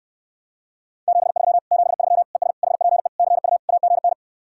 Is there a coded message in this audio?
Yes: Morse code from 1.1 to 4.1 s, heard as a keyed tone, about 69 wpm, 700 Hz. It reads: 73 73 ES HPE CU AGN